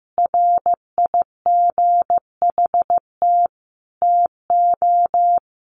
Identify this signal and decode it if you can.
Morse code, heard as a keyed tone, about 15 wpm, 700 Hz. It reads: RIGHT TO
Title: Sentences from Top 1400 Words 15wpm